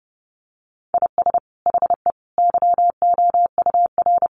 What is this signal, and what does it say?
Morse code, heard as a keyed tone, about 30 wpm, 700 Hz. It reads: IS HE YOUR
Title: Sentences from Top 1000 Words 30wpm